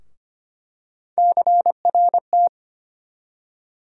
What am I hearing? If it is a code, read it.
Morse code, heard as a keyed tone, about 25 wpm, 700 Hz. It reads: CRT
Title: USA Call Sign Postfixes - 3 Letters 25wpm